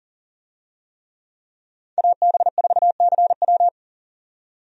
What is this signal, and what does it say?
Morse code, heard as a keyed tone, about 40 wpm, 700 Hz. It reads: AB4CW
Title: USA Call Signs - 5 Characters - 40wpm